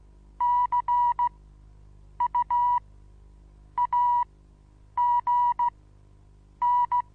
Morse code, heard as a keyed tone, about 15 wpm, spaced as for 7 wpm, 970 Hz. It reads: CUAGN
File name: Onlinelehrgang / morsen / DJ4UF Morsen25